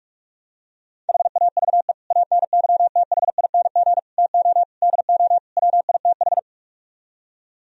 Morse code, heard as a keyed tone, about 49 wpm, 700 Hz. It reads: HAVE ANYTHING TO DO WITH